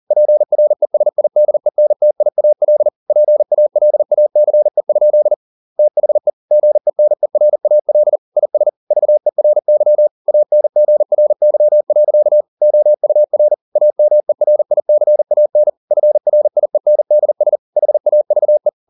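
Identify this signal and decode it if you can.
Morse code, heard as a keyed tone, about 40 wpm, 600 Hz. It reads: PRESIDENTIAL PALACE? THE GENERAL IS VERY ANGRY. OUR AMERICAN FRIENDS HAVE
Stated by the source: Morse code at 40 wpm, practice text, CW med Gnister.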